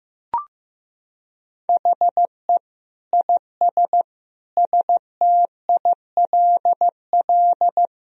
Morse code, heard as a keyed tone, about 15 wpm, 700 Hz. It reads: HE IS STILL
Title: Sentences from Top 1000 Words - Review Part 1 - 15wpm